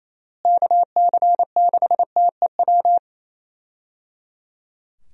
Morse code, heard as a keyed tone, about 28 wpm, 700 Hz. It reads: KC6TEW
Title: USA Call Signs - 6 Characters - Repeated 3x - 28wpm